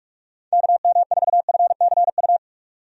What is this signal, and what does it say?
Morse code, heard as a keyed tone, about 42 wpm, 700 Hz. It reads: KM4FCU
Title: USA Call Signs - 6 Characters - 45wpm